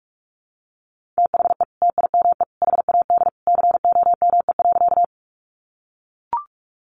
Morse code, heard as a keyed tone, about 44 wpm, 700 Hz. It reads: THE TIME HAD COME.
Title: Sentences from Top 100 Words 45wpm